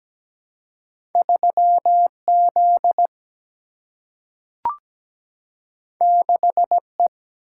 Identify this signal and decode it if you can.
Morse code, heard as a keyed tone, about 17 wpm, 700 Hz. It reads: 3Z 6E
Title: Single-Letter Single-Number 17wpm